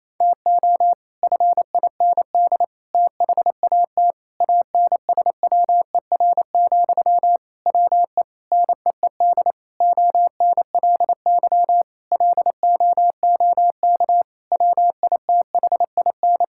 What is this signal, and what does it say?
Morse code, heard as a keyed tone, about 28 wpm, 700 Hz. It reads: TO FIND THAT ANSWER, WE NEED ONLY LOOK WITHIN
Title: Sentences from Top 1600 Words 28wpm